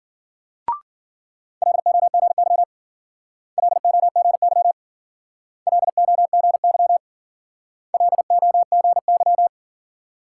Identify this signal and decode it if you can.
Morse code, heard as a keyed tone, about 52 wpm, 700 Hz. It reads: LOGY LOGY LOGY LOGY